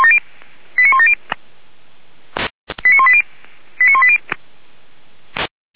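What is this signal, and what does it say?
Five-tone selective call (ZVEI), heard as quick runs of five tones, tones of about 70 ms. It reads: 89170 89180 89180
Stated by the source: ZVEI five-tone call 89138 received on an SDR, recorded by Feuerwehr Orlamünde.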